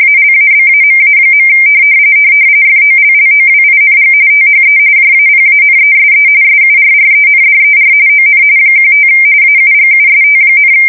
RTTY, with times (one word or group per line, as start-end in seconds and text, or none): 0.0-1.5 ALABAMA.
2.7-4.1 FOLLOWING
4.3-4.8 THE
5.0-6.3 JAPANESE
6.4-7.9 SURRENDER
8.1-8.4 ON
8.8-9.1 15
9.4-10.7 AUGUST,